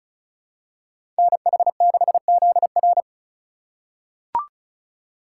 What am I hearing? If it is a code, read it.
Morse code, heard as a keyed tone, about 36 wpm, 700 Hz. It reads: NH6ZR
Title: USA Call Signs - 5 Characters - 4x Character Spacing - 35wpm